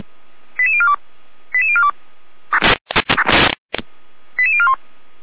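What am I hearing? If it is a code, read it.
Five-tone selective call (ZVEI), heard as quick runs of five tones, tones of about 70 ms. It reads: 88042 88042 88041
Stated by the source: ZVEI five-tone selective call 88150 received by SDR.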